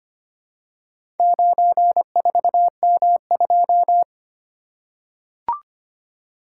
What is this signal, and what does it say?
Morse code, heard as a keyed tone, about 25 wpm, 700 Hz. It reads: 94M2
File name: practice / 25wpm / Four Letter-Number Combinations - 25wpm